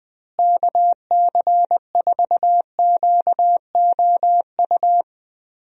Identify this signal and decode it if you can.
Morse code, heard as a keyed tone, about 21 wpm, 700 Hz. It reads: KC4QOU
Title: USA Call Signs - 6 Characters - Repeated 2x - 20wpm